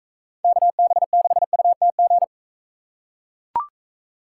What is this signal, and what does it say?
Morse code, heard as a keyed tone, about 42 wpm, 700 Hz. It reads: KB6UTG